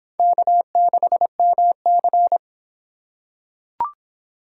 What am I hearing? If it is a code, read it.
Morse code, heard as a keyed tone, about 25 wpm, 700 Hz. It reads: K6MC